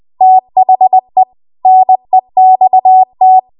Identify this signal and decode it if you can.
Morse code, heard as a keyed tone, about 20 wpm, 750 Hz. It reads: THE NEXT